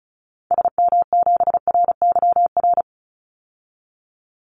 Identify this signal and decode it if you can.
Morse code, heard as a keyed tone, about 34 wpm, 700 Hz. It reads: SM7RYR
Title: Call Signs Repeated 2x - 35wpm